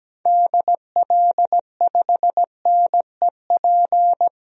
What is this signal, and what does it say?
Morse code, heard as a keyed tone, about 17 wpm, 700 Hz. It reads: DL5NEP
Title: DXCC-100 Call Signs - 6 Characters - 17wpm